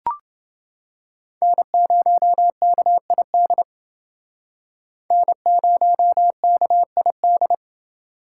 Morse code, heard as a keyed tone, about 28 wpm, 700 Hz. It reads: N0KID N0KID